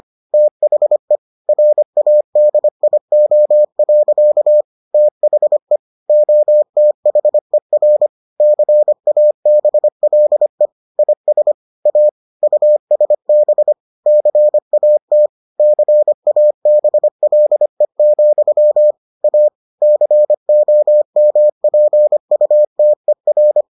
Morse code, heard as a keyed tone, about 25 wpm, 600 Hz. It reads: THE RADIO. THE OTHER CABLE IS A USB CAT CABLE, A COMPUTER